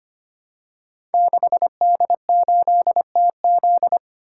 Morse code, heard as a keyed tone, about 25 wpm, 700 Hz. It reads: 6D8TZ
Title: Code Groups - 5 Letter:Numbers - 25wpm